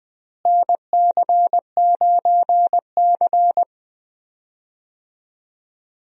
Morse code, heard as a keyed tone, about 20 wpm, 700 Hz. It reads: NC9C
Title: USA Call Signs - 4 Characters - Repeated 3x - 20wpm